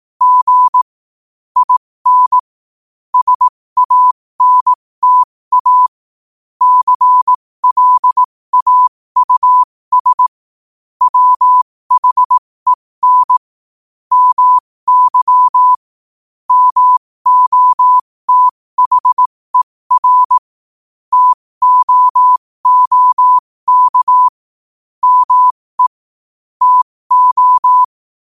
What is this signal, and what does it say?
Morse code, heard as a keyed tone, about 18 wpm, spaced as for 15 wpm, 1000 Hz. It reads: G IN SANTA CLAUS WHEN MY MOTHER TOOK ME TO
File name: 15wpm